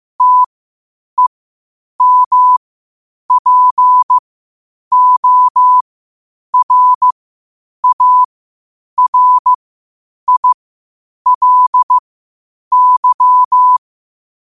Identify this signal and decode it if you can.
Morse code, heard as a keyed tone, about 15 wpm, spaced as for 8 wpm, 1000 Hz. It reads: TEMPORARILY
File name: qotdfiles / 8wpm